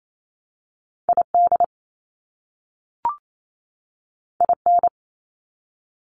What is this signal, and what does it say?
Morse code, heard as a keyed tone, about 28 wpm, 700 Hz. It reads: ID IN
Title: US States 28wpm